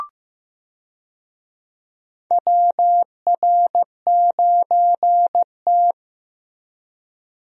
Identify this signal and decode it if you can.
Morse code, heard as a keyed tone, about 15 wpm, 700 Hz. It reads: WR9T
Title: USA Call Signs - 4 Characters - Repeated 2x - 15wpm